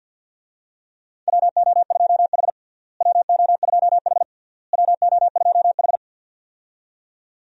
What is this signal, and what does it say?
Morse code, heard as a keyed tone, about 51 wpm, 700 Hz. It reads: WO2H WO2H WO2H
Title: USA Call Signs - 4 Characters - Repeated 3x - 50wpm